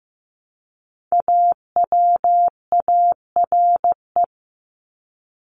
Morse code, heard as a keyed tone, about 15 wpm, 700 Hz. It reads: AWARE
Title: Top 1001-1250 Words 15wpm